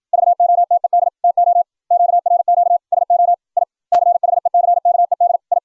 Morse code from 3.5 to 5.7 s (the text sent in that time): I CHECKED I